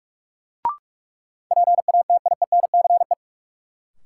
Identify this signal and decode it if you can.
Morse code, heard as a keyed tone, about 44 wpm, 700 Hz. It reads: PATIENCE
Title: Sentences from Top 1300 Words - New Word Drill - 45wpm